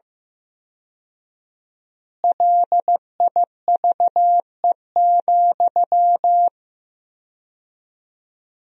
Morse code, heard as a keyed tone, about 16 wpm, 700 Hz. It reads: LIVE,
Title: 4x Word-Spacing - Sentences from Top 200 Words - 15wpm